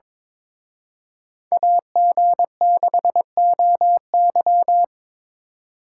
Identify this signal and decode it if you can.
Morse code, heard as a keyed tone, about 26 wpm, 700 Hz. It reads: AG6OY
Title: Speed Racing - USA 2x2 Call Signs 22wpm